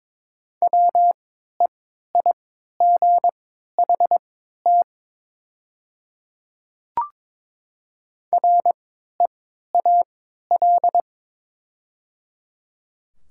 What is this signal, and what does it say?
Morse code, heard as a keyed tone, about 22 wpm, spaced as for 12 wpm, 700 Hz. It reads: WEIGHT REAL
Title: 3x Character Spacing - Top 500 Words - 22wpm